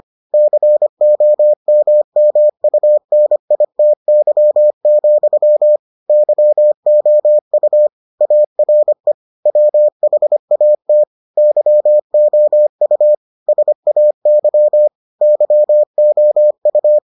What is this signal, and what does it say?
Morse code, heard as a keyed tone, about 25 wpm, 600 Hz. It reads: COMMUNITY, YOU ARE WHAT YOU SAY YOU